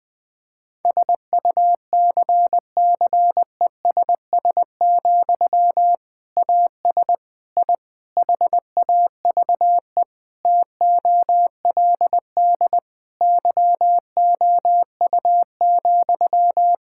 Morse code, heard as a keyed tone, about 21 wpm, 700 Hz. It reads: SUCCESS, AS I HAVE TOLD YOU,
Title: Sentences from Top 1000 Words - Review Part 6 - 20wpm